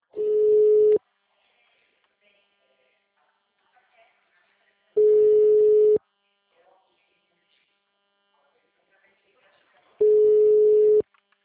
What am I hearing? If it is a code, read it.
Ringback tone (Europe, 425 Hz), with speech.
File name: out- -2003- - - .27056